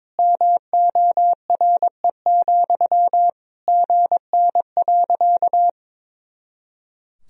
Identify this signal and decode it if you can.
Morse code, heard as a keyed tone, about 21 wpm, 700 Hz. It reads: MORE, GN.
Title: Sentences from Top 300 Words and Abbreviations 22wpm